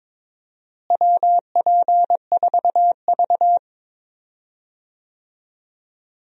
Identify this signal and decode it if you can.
Morse code, heard as a keyed tone, about 22 wpm, 700 Hz. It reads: WP4V
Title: USA Call Signs - 4 Characters - Repeated 3x - 22wpm